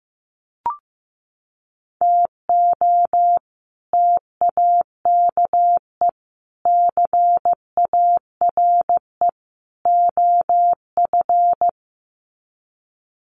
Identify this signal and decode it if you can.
Morse code, heard as a keyed tone, about 15 wpm, 700 Hz. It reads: TO TAKE CARE OF
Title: Sets of 4 Words 15wpm